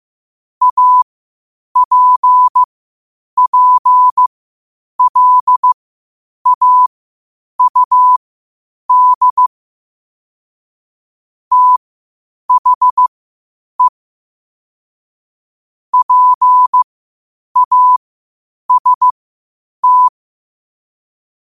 Morse code, heard as a keyed tone, about 15 wpm, spaced as for 8 wpm, 1000 Hz. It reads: APPLAUD THE PAST
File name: qotdfiles / 8wpm